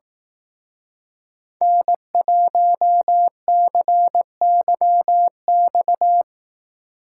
Morse code, heard as a keyed tone, about 18 wpm, 700 Hz. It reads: N1CYX
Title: Speed Racing - USA 1x3 Call Signs 15wpm